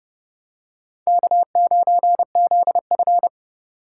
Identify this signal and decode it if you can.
Morse code, heard as a keyed tone, about 30 wpm, 700 Hz. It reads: K9ZF